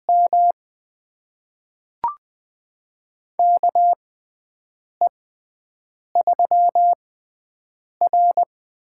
Morse code, heard as a keyed tone, about 20 wpm, spaced as for 7 wpm, 700 Hz. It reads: M KE3R
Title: USA Call Signs - 4 Characters - 6x Character Spacing - 20wpm